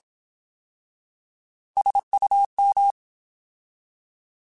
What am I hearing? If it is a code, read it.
Morse code, heard as a keyed tone, about 27 wpm, 800 Hz. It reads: SUM
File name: Alphabetical / Alphabetical mp3s / Ss @25